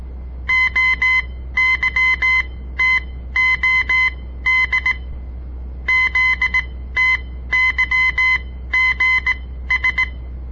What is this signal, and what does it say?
Morse code, heard as a keyed tone, about 18 wpm, spaced as for 13 wpm, 2050 Hz. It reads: OYTOD ZTYGS
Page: On the air Morse code practice from the Southern Montana Amatuer Radio Association; all code, no read back